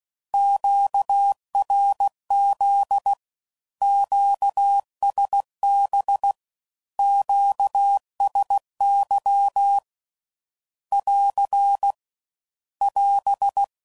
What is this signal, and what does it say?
Morse code, heard as a keyed tone, about 16 wpm, 800 Hz. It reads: QRZ QSB QSY <AR> <AS>
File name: CW Code Groups @15WPM / Q Calls / Q Calls Once